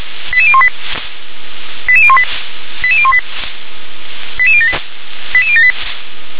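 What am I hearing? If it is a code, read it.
Five-tone selective call (ZVEI), heard as quick runs of five tones, tones of about 70 ms. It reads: 88018 88018 88018 88078 88078
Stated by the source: ZVEI five-tone call 88185 received with an SDR.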